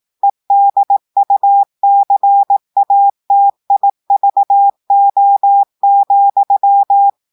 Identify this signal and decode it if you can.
Morse code, heard as a keyed tone, about 18 wpm, 800 Hz. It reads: EDUCATIVO,
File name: audios / portugais / portugais texte en clair 1 vitesse 18 mots minutes